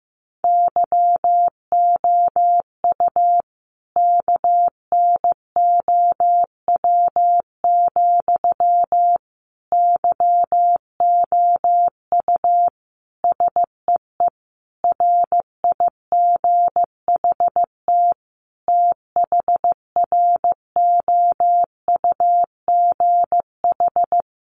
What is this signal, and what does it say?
Morse code, heard as a keyed tone, about 15 wpm, 700 Hz. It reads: YOU KNOW, YOU SEE RIGHT THROUGH
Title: Sentences from Top 200 Words 15wpm